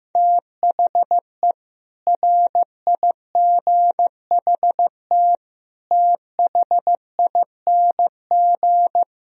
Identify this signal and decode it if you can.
Morse code, heard as a keyed tone, about 16 wpm, 700 Hz. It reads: THE RIGHT THING
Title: Speed-Racing - Sets of 5 Words 15wpm